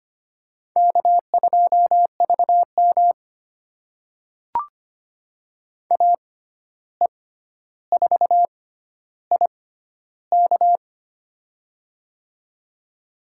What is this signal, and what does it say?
Morse code, heard as a keyed tone, about 25 wpm, spaced as for 12 wpm, 700 Hz. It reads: K2VM AE4IK
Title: USA Call Signs - 6x Character Spacing - 25wpm